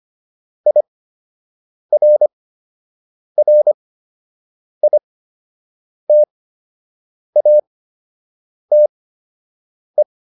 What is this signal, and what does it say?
Morse code, heard as a keyed tone, about 25 wpm, spaced as for 7 wpm, 600 Hz. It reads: IRRITATE